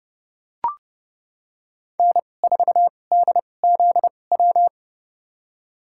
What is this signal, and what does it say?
Morse code, heard as a keyed tone, about 30 wpm, spaced as for 22 wpm, 700 Hz. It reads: N4DZW
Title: USA Call Signs - 5 Characters - 2x Character Spacing - 30wpm